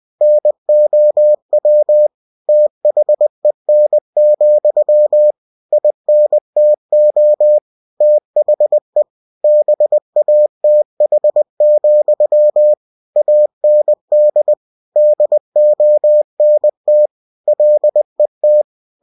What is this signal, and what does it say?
Morse code, heard as a keyed tone, about 20 wpm, 600 Hz. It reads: NOW THEN, INTO THE BATH, AND DONT LET
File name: cw-books / winnie-the-pooh / winnie-20wpm-page-0072